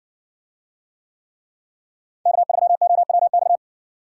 Morse code, heard as a keyed tone, about 60 wpm, 700 Hz. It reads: K3OWX